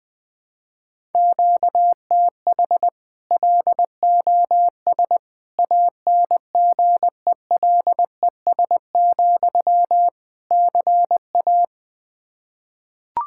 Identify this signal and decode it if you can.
Morse code, heard as a keyed tone, about 20 wpm, 700 Hz. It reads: QTH LOS ANGELES, CA
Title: Practice QSO - Exchange 1 - Beginning - 20wpm